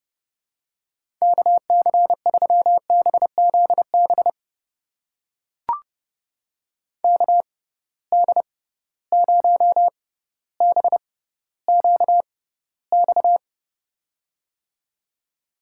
Morse code, heard as a keyed tone, about 30 wpm, spaced as for 15 wpm, 700 Hz. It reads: KC3BZB KD0BQX
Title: USA Call Signs - 6 Characters - 6x Character Spacing - 30wpm